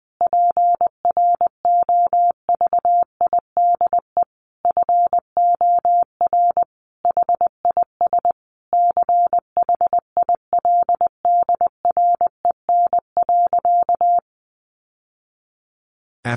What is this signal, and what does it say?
Morse code, heard as a keyed tone, about 19 wpm, 700 Hz. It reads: PROVIDE FOR HIS CHILDREN.
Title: Sentences from Top 1100 Words 20wpm